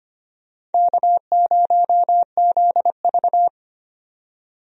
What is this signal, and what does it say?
Morse code, heard as a keyed tone, about 25 wpm, 700 Hz. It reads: K0ZV
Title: USA Call Signs - 4 Characters - 25wpm